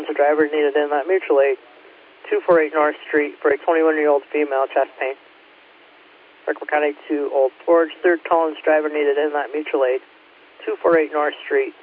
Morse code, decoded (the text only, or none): none